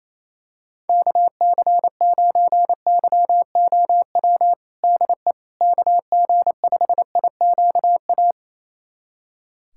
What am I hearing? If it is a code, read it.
Morse code, heard as a keyed tone, about 28 wpm, 700 Hz. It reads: KC9YOW DE KG5IQA
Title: USA Callsign DE Callsign - 6 Characters - 28wpm